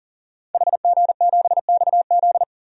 Morse code, heard as a keyed tone, about 40 wpm, 700 Hz. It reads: HG7XZ
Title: DXCC-100 Call Signs - 5 Characters - 40wpm